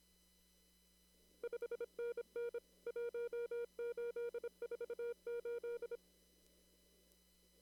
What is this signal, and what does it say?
Morse code, heard as a keyed tone, about 29 wpm, 470 Hz. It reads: 5NN 1848
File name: CW / kt5j wpx cw-op-k5pi